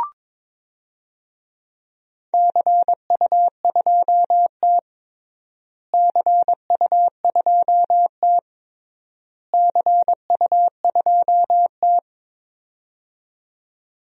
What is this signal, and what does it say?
Morse code, heard as a keyed tone, about 22 wpm, 700 Hz. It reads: CU2T CU2T CU2T
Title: DXCC-100 Call Signs - 4 Characters - Repeated 3x - 22wpm